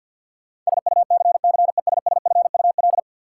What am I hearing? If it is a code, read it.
Morse code, heard as a keyed tone, about 50 wpm, 700 Hz. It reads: SUCCESSFUL
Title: Sentences from Top 900 Words - New Word Drill - 50wpm